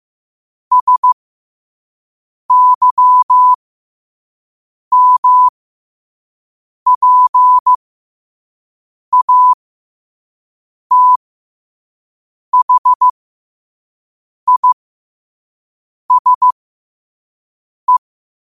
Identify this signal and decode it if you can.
Morse code, heard as a keyed tone, about 15 wpm, spaced as for 5 wpm, 1000 Hz. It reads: SYMPATHISE
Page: New quotes every day in morse code at 5 Words per minute.